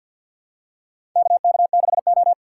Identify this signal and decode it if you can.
Morse code, heard as a keyed tone, about 53 wpm, 700 Hz. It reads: KK6O
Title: USA Call Signs - 4 Characters - 3x Character Spacing - 50wpm